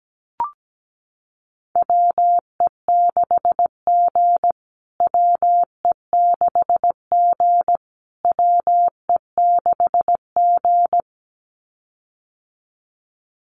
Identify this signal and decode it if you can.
Morse code, heard as a keyed tone, about 17 wpm, 700 Hz. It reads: WE6G WE6G WE6G
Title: USA Call Signs Repeated 3x 17wpm